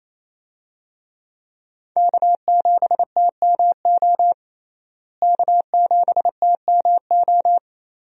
Morse code, heard as a keyed tone, about 28 wpm, 700 Hz. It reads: K7TMO K7TMO
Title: USA Call Signs - 5 Characters - Repeated 2x - 28wpm